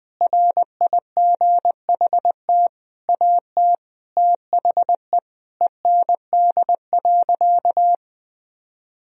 Morse code, heard as a keyed tone, about 20 wpm, 700 Hz. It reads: RIGHT AT THE END.
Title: Sentences from Top 1000 Words - Review Part 2 - 20wpm